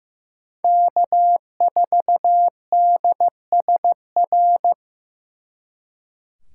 Morse code, heard as a keyed tone, about 18 wpm, 700 Hz. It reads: K4DSR
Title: Speed Racing - USA 1x3 Call Signs 15wpm